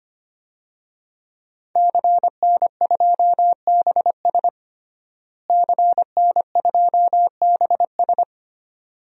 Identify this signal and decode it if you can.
Morse code, heard as a keyed tone, about 25 wpm, 700 Hz. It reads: CN2BS CN2BS
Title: DXCC-100 Call Signs - 5 Characters - Repeated 2x - 25wpm